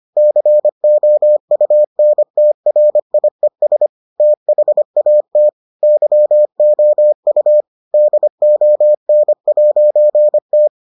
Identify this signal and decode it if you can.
Morse code, heard as a keyed tone, about 25 wpm, 600 Hz. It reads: COUNTRIES THAT YOU DON'T